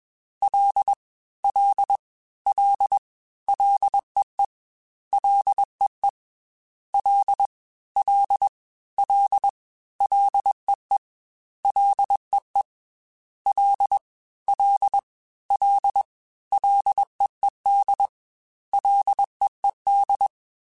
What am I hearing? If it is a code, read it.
Morse code, heard as a keyed tone, about 21 wpm, 800 Hz. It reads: L L L LEE LEE L L L LEE LEE L L L LEED LEED
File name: Ls / Single L m3s / Single L Prefixe 3x2 Twice @20